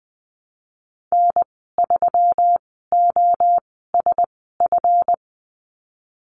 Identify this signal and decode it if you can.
Morse code, heard as a keyed tone, about 20 wpm, spaced as for 14 wpm, 700 Hz. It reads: N3OSF
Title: 173 Lesson 28g - Number 9 - Call Signs 20wpm